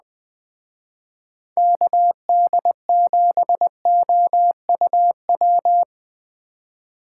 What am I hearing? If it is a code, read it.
Morse code, heard as a keyed tone, about 20 wpm, 700 Hz. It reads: KD7OUW
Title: Speed Racing - USA 2x3 Call Signs 17wpm